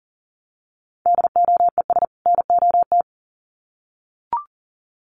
Morse code, heard as a keyed tone, about 40 wpm, 700 Hz. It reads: DOES NOT
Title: Sets of 2 Words 40wpm